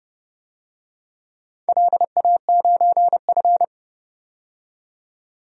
Morse code, heard as a keyed tone, about 30 wpm, 700 Hz. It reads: LA9F
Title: DXCC-100 Call Signs - 4 Characters - 30wpm